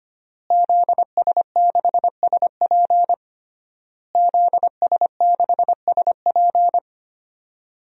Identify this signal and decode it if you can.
Morse code, heard as a keyed tone, about 25 wpm, 700 Hz. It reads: ZS6SP ZS6SP
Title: DXCC-100 Call Signs - 5 Characters - Repeated 3x - 25wpm